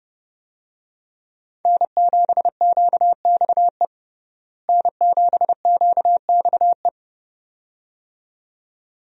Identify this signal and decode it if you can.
Morse code, heard as a keyed tone, about 30 wpm, 700 Hz. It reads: N7QXE N7QXE
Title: USA Call Signs - 5 Characters - Repeated 2x - 30wpm